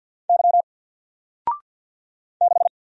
Morse code, heard as a keyed tone, about 50 wpm, 700 Hz. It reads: Y 6
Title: Single Letter-Number Alt 50wpm